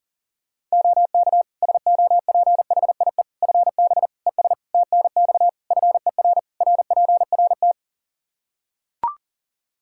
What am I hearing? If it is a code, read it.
Morse code, heard as a keyed tone, about 37 wpm, 700 Hz. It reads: OK SOPHIE FB ES TNX FER RPRT
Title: Practice QSO - Exchange 2 - Info - 40wpm